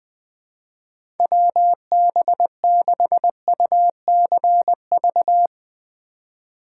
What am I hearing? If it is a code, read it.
Morse code, heard as a keyed tone, about 20 wpm, 700 Hz. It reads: WB6UCV
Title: USA Call Signs - 8x Character Spacing - 20wpm